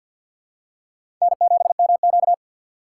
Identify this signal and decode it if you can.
Morse code, heard as a keyed tone, about 51 wpm, 700 Hz. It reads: N7MQ
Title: USA Call Signs - 4 Characters - 2x Character Spacing - 50wpm